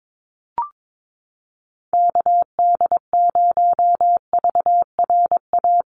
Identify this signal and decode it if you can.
Morse code, heard as a keyed tone, about 22 wpm, 700 Hz. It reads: KD0VRA
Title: USA Call Signs Repeated 2x 22wpm